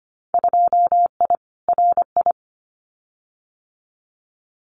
Morse code, heard as a keyed tone, about 25 wpm, 700 Hz. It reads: 2I RI
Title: Winter Field Day Exchanges Encore 25wpm